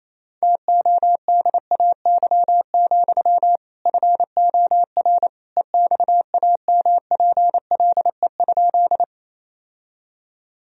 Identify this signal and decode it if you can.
Morse code, heard as a keyed tone, about 28 wpm, 700 Hz. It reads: TODAY, FOR EXAMPLE?
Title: Sentences from Top 1000 Words - Review Part 1 - 28wpm